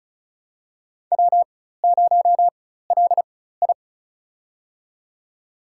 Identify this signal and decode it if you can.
Morse code, heard as a keyed tone, about 36 wpm, spaced as for 16 wpm, 700 Hz. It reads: W0LI
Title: USA Call Signs - 4x Character Spacing - 35wpm